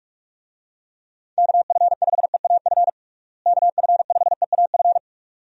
Morse code, heard as a keyed tone, about 42 wpm, 700 Hz. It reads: KF5EAF KF5EAF
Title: USA Call Signs - 6 Characters - Repeated 2x - 45wpm